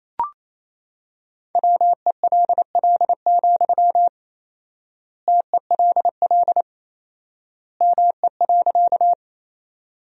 Morse code, heard as a keyed tone, about 28 wpm, 700 Hz. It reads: WELL, TELL ME.
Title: 4x Word-Spacing - Sentences from Top 200 Words - 28wpm